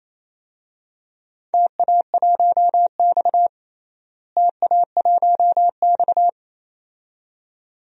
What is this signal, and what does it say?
Morse code, heard as a keyed tone, about 28 wpm, 700 Hz. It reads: TA1X TA1X
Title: DXCC-100 Call Signs - 4 Characters - Repeated 2x - 28wpm